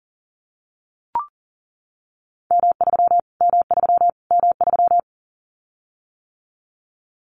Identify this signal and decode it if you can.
Morse code, heard as a keyed tone, about 40 wpm, 700 Hz. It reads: M3 M3 M3
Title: Callsign Prefixes for Top 25 DXCC - Repeated 3x - 40wpm